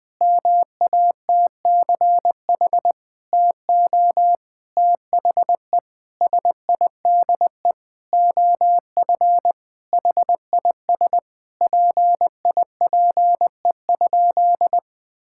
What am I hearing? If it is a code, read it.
Morse code, heard as a keyed tone, about 21 wpm, 700 Hz. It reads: MATCH TO THE SIDE OF HIS PIPE?
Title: Sentences from Top 1600 Words 20wpm